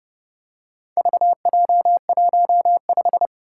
Morse code, heard as a keyed tone, about 29 wpm, 700 Hz. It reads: VJ15